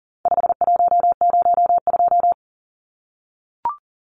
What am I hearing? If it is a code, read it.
Morse code, heard as a keyed tone, about 40 wpm, 700 Hz. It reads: 5102
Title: Numbers - 4-Digit 40wpm